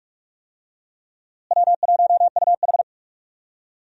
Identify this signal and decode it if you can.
Morse code, heard as a keyed tone, about 42 wpm, 700 Hz. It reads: W1UH
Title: USA Call Signs - 4 Characters - Repeated 2x - 45wpm